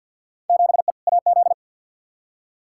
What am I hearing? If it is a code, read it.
Morse code, heard as a keyed tone, about 53 wpm, 700 Hz. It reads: 7E AZ
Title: Field Day Exchanges Encore - 50wpm